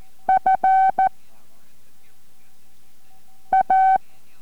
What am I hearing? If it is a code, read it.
Morse code, heard as a keyed tone, about 14 wpm, 760 Hz. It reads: F A